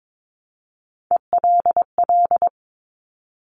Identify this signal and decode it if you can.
Morse code, heard as a keyed tone, about 22 wpm, 700 Hz. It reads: ELL
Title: Top Three Letter Combinations in Words 22wpm